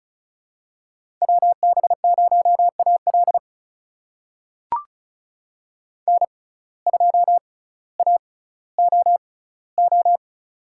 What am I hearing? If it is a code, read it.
Morse code, heard as a keyed tone, about 36 wpm, spaced as for 18 wpm, 700 Hz. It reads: WB0AL N2AOO
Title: USA Call Signs - 5 Characters - 6x Character Spacing - 35wpm